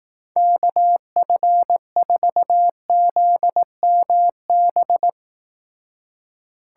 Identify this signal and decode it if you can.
Morse code, heard as a keyed tone, about 18 wpm, 700 Hz. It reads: KF4ZMB
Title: Speed Racing - USA 2x3 Call Signs 15wpm